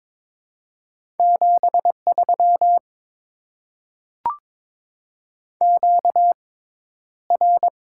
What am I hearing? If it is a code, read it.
Morse code, heard as a keyed tone, about 22 wpm, 700 Hz. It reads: 7 3 Q R
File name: Top 20 QSO Elements - 6x Character Spacing - 22wpm